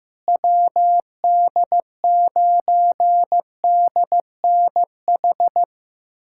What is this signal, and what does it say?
Morse code, heard as a keyed tone, about 15 wpm, 700 Hz. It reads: WD9DNH
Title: USA Callsign DE Callsign - 6 Characters - 15wpm